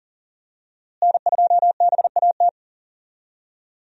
Morse code, heard as a keyed tone, about 40 wpm, 700 Hz. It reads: N2BAT
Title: USA Call Signs - 5 Characters - Repeated 2x - 40wpm